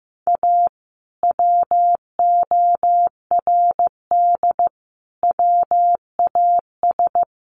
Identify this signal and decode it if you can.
Morse code, heard as a keyed tone, about 15 wpm, 700 Hz. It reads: A WORD WAS
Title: Sentences from Top 1400 Words 15wpm